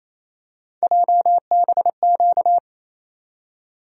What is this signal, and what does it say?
Morse code, heard as a keyed tone, about 28 wpm, 700 Hz. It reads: JBQ
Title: USA Call Sign Postfixes - 3 Letters 28wpm